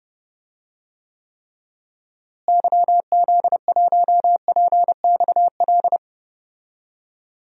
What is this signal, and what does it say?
Morse code, heard as a keyed tone, about 32 wpm, 700 Hz. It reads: YZ1PXL
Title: DXCC-100 Call Signs - 6 Characters - Repeated 3x - 30wpm